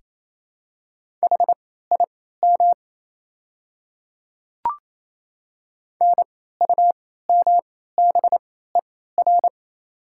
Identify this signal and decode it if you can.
Morse code, heard as a keyed tone, about 27 wpm, spaced as for 16 wpm, 690 Hz. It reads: HIM NUMBER
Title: 3x Character Spacing - Top 100 Words - 28wpm